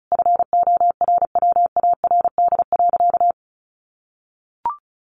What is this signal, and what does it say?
Morse code, heard as a keyed tone, about 34 wpm, 700 Hz. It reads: FORWARD.
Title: Sentences from Top 1300 Words 35wpm